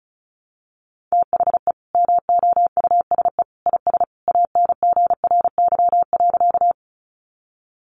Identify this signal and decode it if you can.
Morse code, heard as a keyed tone, about 34 wpm, 700 Hz. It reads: THE MOUSE IS ANGRY.
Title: Easy Phrases 35wpm